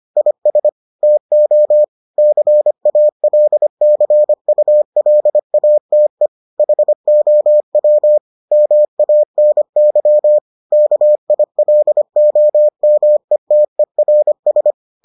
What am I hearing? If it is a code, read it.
Morse code, heard as a keyed tone, about 25 wpm, 600 Hz. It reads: IS TO CALCULATE HOW MANY KILOMETERS